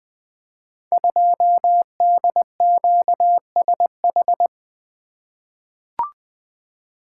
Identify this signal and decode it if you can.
Morse code, heard as a keyed tone, about 20 wpm, 700 Hz. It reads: 2DQSH
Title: Code Groups - 5 Letter:Numbers - 20wpm